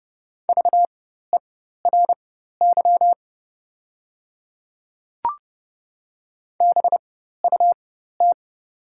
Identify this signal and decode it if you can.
Morse code, heard as a keyed tone, about 30 wpm, spaced as for 14 wpm, 700 Hz. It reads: VERY BUT
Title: 4x Character Spacing - Top 300 Words - 30wpm